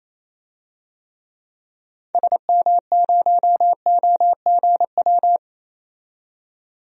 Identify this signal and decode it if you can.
Morse code, heard as a keyed tone, about 28 wpm, 700 Hz. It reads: SM0OGW
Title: DXCC-100 Call Signs - 6 Characters - 28wpm